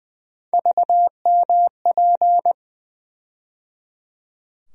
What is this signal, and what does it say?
Morse code, heard as a keyed tone, about 20 wpm, 700 Hz. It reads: VMP
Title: USA Call Sign Postfixes - 3 Letters 20wpm